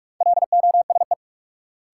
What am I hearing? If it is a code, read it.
Morse code, heard as a keyed tone, about 42 wpm, 700 Hz. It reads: LOSE
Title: Sentences from Top 1100 Words - New Word Drill - 45wpm